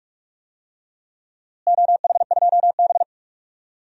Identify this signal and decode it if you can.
Morse code, heard as a keyed tone, about 42 wpm, 700 Hz. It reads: OH2B
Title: DXCC-100 Call Signs - 4 Characters - Repeated 2x - 45wpm